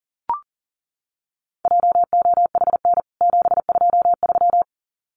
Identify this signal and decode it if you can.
Morse code, heard as a keyed tone, about 40 wpm, 700 Hz. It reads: JOHN 723
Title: CWT Exchange 40wpm